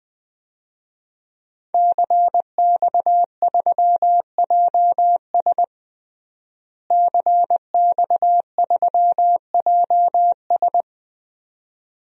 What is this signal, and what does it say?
Morse code, heard as a keyed tone, about 20 wpm, 700 Hz. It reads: CX3JS CX3JS
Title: DXCC-100 Call Signs - 5 Characters - Repeated 2x - 20wpm